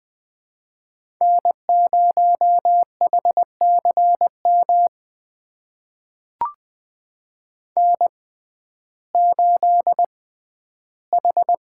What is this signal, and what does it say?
Morse code, heard as a keyed tone, about 20 wpm, spaced as for 12 wpm, 700 Hz. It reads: N0HCM N8H
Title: USA Call Signs - 5 Characters - 6x Character Spacing - 20wpm